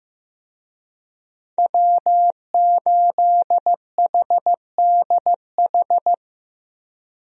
Morse code, heard as a keyed tone, about 15 wpm, 700 Hz. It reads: W8HDH
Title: USA Call Signs - 5 Characters - Repeated 2x - 15wpm